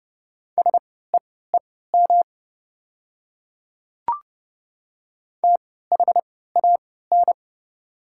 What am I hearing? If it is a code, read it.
Morse code, heard as a keyed tone, about 31 wpm, spaced as for 17 wpm, 700 Hz. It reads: SEEM THAN
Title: 3x Character Spacing - Top 500 Words - 30wpm